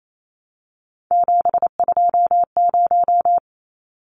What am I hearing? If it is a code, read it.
Morse code, heard as a keyed tone, about 28 wpm, 700 Hz. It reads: 720
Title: Numbers - 3-Digit 28wpm